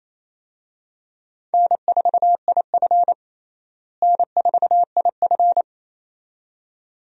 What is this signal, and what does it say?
Morse code, heard as a keyed tone, about 28 wpm, 700 Hz. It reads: N4IF N4IF
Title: USA Call Signs - 4 Characters - Repeated 2x - 28wpm